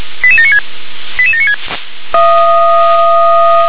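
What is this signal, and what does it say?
Five-tone selective call (ZVEI), heard as quick runs of five tones, tones of about 80 ms. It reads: 88796 88796
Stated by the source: Received on SDR, ZVEI five-tone code 88792, 357 channel.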